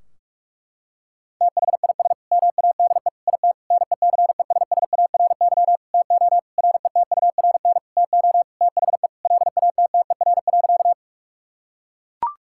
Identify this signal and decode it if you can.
Morse code, heard as a keyed tone, about 41 wpm, 700 Hz. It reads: THIS MADE IT NECESSARY TO RETURN TO THE LATTER.
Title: Sentences from Top 1600 Words 45wpm